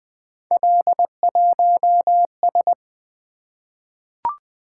Morse code, heard as a keyed tone, about 20 wpm, 700 Hz. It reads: L1S
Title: Three Letter-Number Combinations - 20wpm